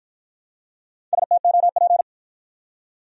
Morse code, heard as a keyed tone, about 53 wpm, 700 Hz. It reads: STOP